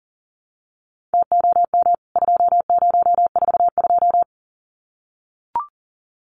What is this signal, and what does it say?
Morse code, heard as a keyed tone, about 40 wpm, 700 Hz. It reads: TOM 2042
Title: CWT Exchange 40wpm